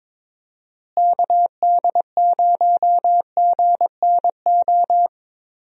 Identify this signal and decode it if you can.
Morse code, heard as a keyed tone, about 22 wpm, 700 Hz. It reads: KD0GNO
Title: USA Call Signs - 6 Characters - Repeated 2x - 22wpm